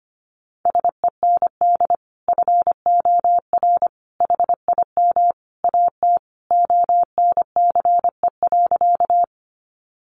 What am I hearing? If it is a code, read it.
Morse code, heard as a keyed tone, about 25 wpm, 700 Hz. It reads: SEND FOR HIM AT ONCE.
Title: Sentences from Top 1500 Words 25wpm